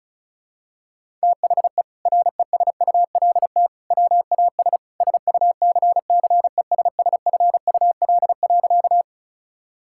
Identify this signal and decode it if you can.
Morse code, heard as a keyed tone, about 36 wpm, 700 Hz. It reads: THE RESULT WAS SUCCESSFUL.
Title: Sentences from Top 1000 Words - Review Part 6 - 35wpm